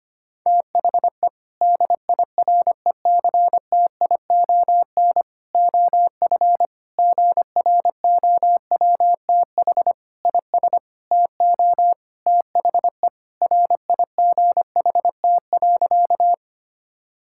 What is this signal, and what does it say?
Morse code, heard as a keyed tone, about 25 wpm, 700 Hz. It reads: THE DIRECTION OF GROWTH IS TO THE RIGHT.
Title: Sentences from Top 1000 Words - Review Part 5 - 25wpm